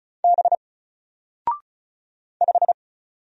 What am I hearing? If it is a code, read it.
Morse code, heard as a keyed tone, about 36 wpm, 700 Hz. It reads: B 5
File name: Single-Letter-Number-35wpm